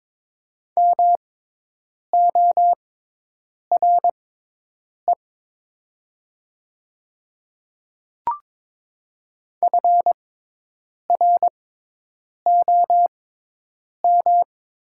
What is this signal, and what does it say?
Morse code, heard as a keyed tone, about 22 wpm, spaced as for 8 wpm, 700 Hz. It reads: MORE FROM
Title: 6x Character Spacing - Top 100 Words - 22wpm